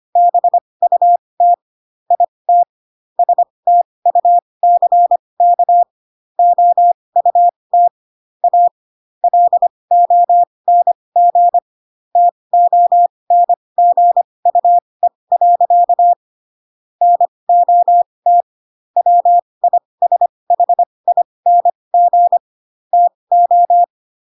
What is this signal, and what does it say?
Morse code, heard as a keyed tone, about 25 wpm, spaced as for 20 wpm, 700 Hz. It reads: BUT IT STUCK OUT A LONG TONGUE. NOT WISHING TO